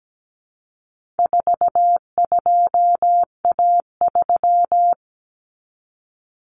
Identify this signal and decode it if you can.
Morse code, heard as a keyed tone, about 17 wpm, 700 Hz. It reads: 42A3
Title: Common Cut Numbers - 4-Digit 17wpm